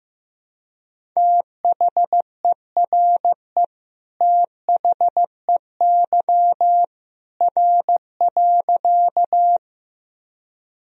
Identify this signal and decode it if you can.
Morse code, heard as a keyed tone, about 15 wpm, 700 Hz. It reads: THERE THEY R.
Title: Sentences from Top 100 Words and Abbreviations 15wpm